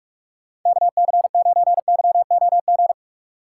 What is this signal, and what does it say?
Morse code, heard as a keyed tone, about 46 wpm, 700 Hz. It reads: KC9YOG
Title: USA Call Signs - 6 Characters - 2x Character Spacing - 45wpm